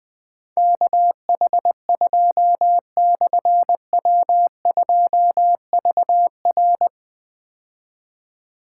Morse code, heard as a keyed tone, about 20 wpm, 700 Hz. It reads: KH2/W2VR
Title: Difficult Call Signs 20wpm